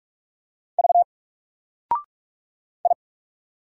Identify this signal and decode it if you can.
Morse code, heard as a keyed tone, about 42 wpm, 700 Hz. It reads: V I
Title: Single Letter-Number 45wpm